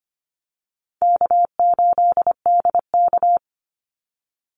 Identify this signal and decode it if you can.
Morse code, heard as a keyed tone, about 25 wpm, 700 Hz. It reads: K8DK
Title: Easy Call Signs Repeated 3x - 25wpm